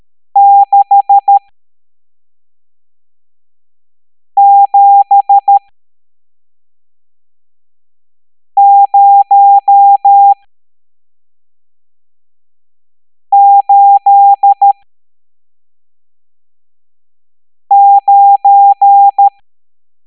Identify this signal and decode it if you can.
Morse code, heard as a keyed tone, about 13 wpm, spaced as for 3 wpm, 800 Hz. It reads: 67089
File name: hamdata / L11-EX01-67890